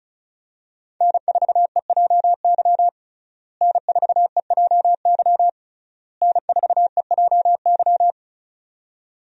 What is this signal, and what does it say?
Morse code, heard as a keyed tone, about 36 wpm, 700 Hz. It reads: N4EJY N4EJY N4EJY
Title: USA Call Signs - 5 Characters - Repeated 3x - 35wpm